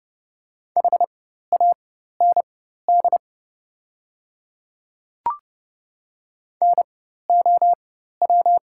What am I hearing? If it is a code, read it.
Morse code, heard as a keyed tone, about 30 wpm, spaced as for 14 wpm, 700 Hz. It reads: HAND NOW
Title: 4x Character Spacing - Top 200 Words - 30wpm